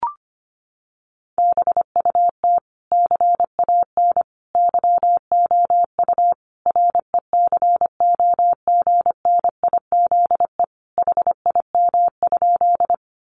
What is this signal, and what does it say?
Morse code, heard as a keyed tone, about 25 wpm, 700 Hz. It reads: BUT CAN YOU RECOGNIZE HIM?
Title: Sentences from Top 1400 Words 25wpm